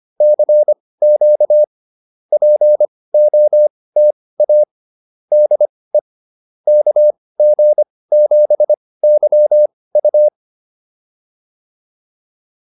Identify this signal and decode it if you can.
Morse code, heard as a keyed tone, about 25 wpm, spaced as for 18 wpm, 600 Hz. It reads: CQ POTA DE KG7YU
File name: POTA108 18